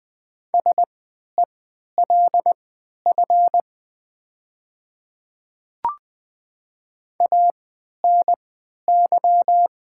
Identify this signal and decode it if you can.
Morse code, heard as a keyed tone, about 20 wpm, spaced as for 11 wpm, 700 Hz. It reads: SELF ANY
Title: 3x Character Spacing - Top 200 Words - 20wpm